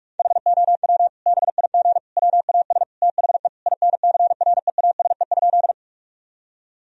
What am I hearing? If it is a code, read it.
Morse code, heard as a keyed tone, about 46 wpm, 700 Hz. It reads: HOW BIG WAS THE INCREASE?
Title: Sentences from Top 1000 Words - Review Part 4 - 45wpm